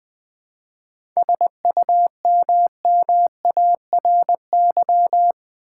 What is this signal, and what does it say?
Morse code, heard as a keyed tone, about 20 wpm, 700 Hz. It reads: SUMMARY
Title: 7 Letter Words - First 3 Letters Given - 20wpm